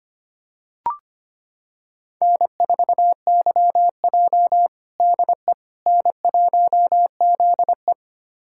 Morse code, heard as a keyed tone, about 25 wpm, 700 Hz. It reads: N4YJ DE N1ZE
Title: USA Callsign DE Callsign - 4 Characters - 25wpm